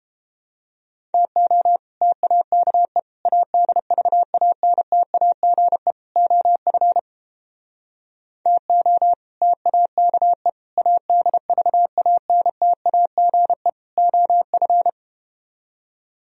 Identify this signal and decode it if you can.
Morse code, heard as a keyed tone, about 31 wpm, 700 Hz. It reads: TO TAKE ADVANTAGE OF TO TAKE ADVANTAGE OF